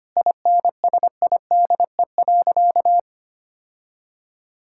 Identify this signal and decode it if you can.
Morse code, heard as a keyed tone, about 25 wpm, 700 Hz. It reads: INSIDE.
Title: Sentences from Top 1000 Words - Review Part 6 - 25wpm